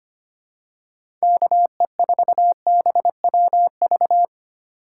Morse code, heard as a keyed tone, about 25 wpm, 700 Hz. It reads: KE4BWV
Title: USA Call Signs - 6 Characters - 8x Character Spacing - 25wpm